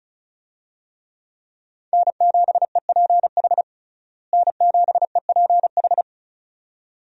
Morse code, heard as a keyed tone, about 34 wpm, 700 Hz. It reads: N7EPH N7EPH
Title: USA Call Signs - 5 Characters - Repeated 2x - 35wpm